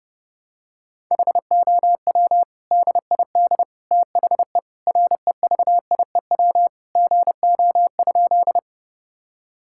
Morse code, heard as a keyed tone, about 30 wpm, 700 Hz. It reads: HOW DID THE REVIEW GO?
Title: Sentences from Top 1000 Words - Review Part 6 - 30wpm